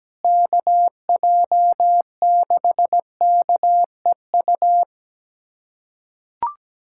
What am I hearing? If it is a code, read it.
Morse code, heard as a keyed tone, about 17 wpm, 700 Hz. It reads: KJ6KEU
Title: USA Callsign DE Callsign - 6 Characters - 17wpm